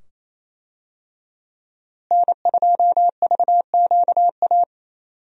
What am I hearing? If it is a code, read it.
Morse code, heard as a keyed tone, about 27 wpm, 700 Hz. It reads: N2VQA
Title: USA Call Signs - 5 Characters - Repeated 2x - 28wpm